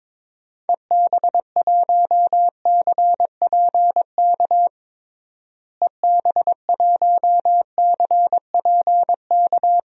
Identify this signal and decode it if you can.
Morse code, heard as a keyed tone, about 22 wpm, 700 Hz. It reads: EB1CPK EB1CPK
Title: DXCC-100 Call Signs - 6 Characters - Repeated 3x - 22wpm